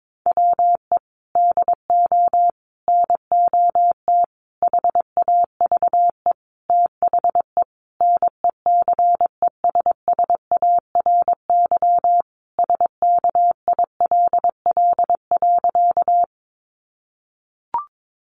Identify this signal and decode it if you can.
Morse code, heard as a keyed tone, about 22 wpm, 700 Hz. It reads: WE DO NOT HAVE THE NECESSARY SKILL.
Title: Sentences from Top 1200 Words 22wpm